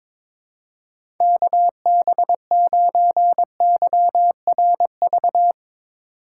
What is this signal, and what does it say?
Morse code, heard as a keyed tone, about 22 wpm, 700 Hz. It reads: KB9YRV
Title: USA Call Signs - 6 Characters - 2x Character Spacing - 22wpm